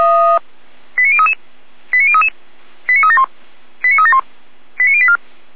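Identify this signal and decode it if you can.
Five-tone selective call (ZVEI), heard as quick runs of five tones, tones of about 80 ms. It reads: 89033 89033 89371 89371 89084
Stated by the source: ZVEI five-tone selective call received by SDR, from the Feuerwehr Orlamünde fire brigade server.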